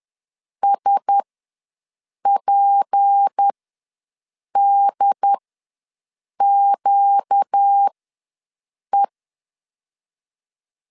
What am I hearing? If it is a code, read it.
Morse code, heard as a keyed tone, about 10 wpm, spaced as for 6 wpm, 800 Hz. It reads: SPDQE
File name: LEZIONI / 6 PKX0 / 6 P ws 10wpm 3751